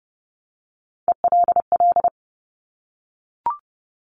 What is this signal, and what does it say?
Morse code, heard as a keyed tone, about 32 wpm, 700 Hz. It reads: ELL
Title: Top Three Letter Combinations in Words 30wpm